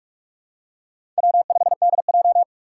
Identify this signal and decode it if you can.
Morse code, heard as a keyed tone, about 44 wpm, 700 Hz. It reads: W5DJ
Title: Four Challenging Letter-Number Combinations 45wpm